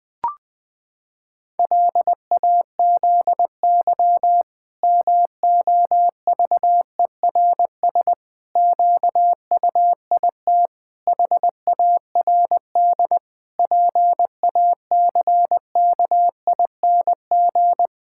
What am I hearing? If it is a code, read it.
Morse code, heard as a keyed tone, about 20 wpm, 700 Hz. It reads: LAZY MOVERS QUIT HARD PACKING
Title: Pangram Challenge v2 - 20wpm